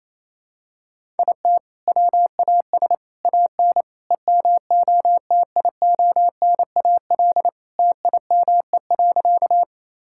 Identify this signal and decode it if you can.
Morse code, heard as a keyed tone, about 27 wpm, 700 Hz. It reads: IT WAS AN EMOTIONAL TIME.
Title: Sentences from Top 1000 Words - Review Part 5 - 28wpm